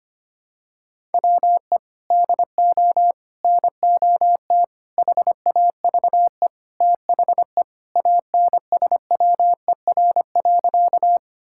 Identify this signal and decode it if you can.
Morse code, heard as a keyed tone, about 25 wpm, 700 Hz. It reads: WE DO NOT HAVE THE ANSWER.
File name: practice / 25wpm / Sentences with 4-Word Phrases - 25wpm